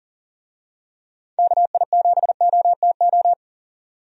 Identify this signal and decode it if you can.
Morse code, heard as a keyed tone, about 40 wpm, 700 Hz. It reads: KI7OTO
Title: USA Call Signs - 6 Characters - 40wpm